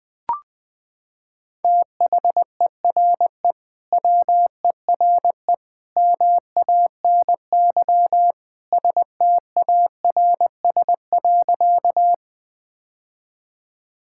Morse code, heard as a keyed tone, about 21 wpm, 700 Hz. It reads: THERE WERE MANY STARS.
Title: Sentences with 2-Word Phrases - 20wpm